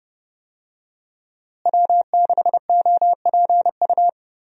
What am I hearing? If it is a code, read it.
Morse code, heard as a keyed tone, about 30 wpm, 700 Hz. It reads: W6OPU
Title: USA Call Signs - 5 Characters - Repeated 3x - 30wpm